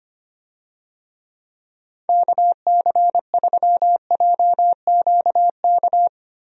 Morse code, heard as a keyed tone, about 25 wpm, 700 Hz. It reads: KC3JQK
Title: USA Call Signs - 6 Characters - Repeated 3x - 25wpm